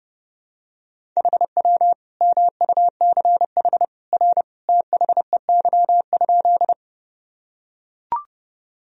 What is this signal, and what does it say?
Morse code, heard as a keyed tone, about 30 wpm, 700 Hz. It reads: HW MUCH R THEY?
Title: Sentences from Top 200 Words and Abbreviations 30wpm